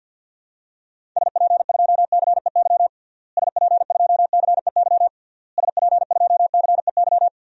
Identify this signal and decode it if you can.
Morse code, heard as a keyed tone, about 52 wpm, 700 Hz. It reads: SP2CEY SP2CEY SP2CEY
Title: DXCC-100 Call Signs - 6 Characters - Repeated 3x - 50wpm